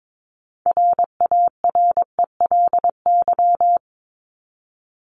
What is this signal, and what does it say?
Morse code, heard as a keyed tone, about 22 wpm, 700 Hz. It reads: RARELY
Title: LY Suffix - Top 100 Words with Base Words 22wpm